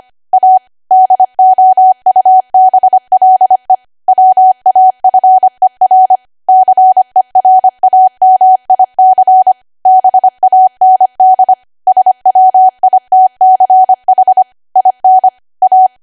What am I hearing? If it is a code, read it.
Morse code, heard as a keyed tone, about 25 wpm, 750 Hz. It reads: A DOUBLE WAFER CERAMIC BAND SWITCH IN A
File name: Archive / 25 WPM / 25WPM